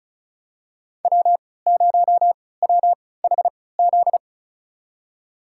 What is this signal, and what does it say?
Morse code, heard as a keyed tone, about 36 wpm, spaced as for 20 wpm, 700 Hz. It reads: W0WHZ
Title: USA Call Signs - 3x Character Spacing - 35wpm